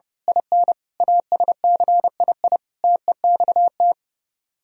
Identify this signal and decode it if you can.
Morse code, heard as a keyed tone, about 30 wpm, 700 Hz. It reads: IN ASCII TEXT
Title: Speed-Racing - Sets of 5 Words 30wpm